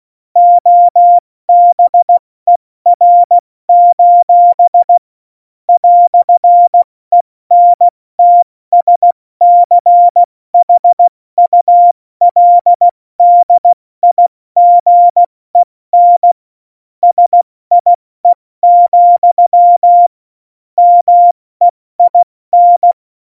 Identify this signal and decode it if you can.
Morse code, heard as a keyed tone, about 16 wpm, 700 Hz. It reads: OBER: "ENTSCHULDIGEN SIE, MEIN
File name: Morsekurs / Witze / Witz 16. April 2019, 70 F80